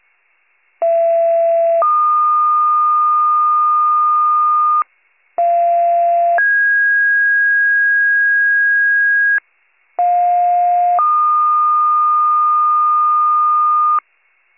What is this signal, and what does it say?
Two-tone page, heard as a one-second tone, then a three-second tone, about 670 then 1220 Hz, about 690 then 1650 Hz, about 710 then 1220 Hz. It is heard from 0.8 to 14.0 s.